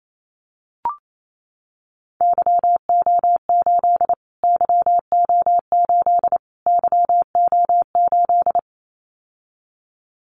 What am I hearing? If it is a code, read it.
Morse code, heard as a keyed tone, about 28 wpm, 700 Hz. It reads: YO8 YO8 YO8
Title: Callsign Prefixes for Top 25 DXCC - Repeated 3x - 28wpm